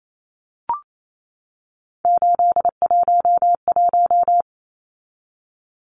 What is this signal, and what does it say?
Morse code, heard as a keyed tone, about 28 wpm, 700 Hz. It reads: 811
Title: Numbers - 3-Digit 28wpm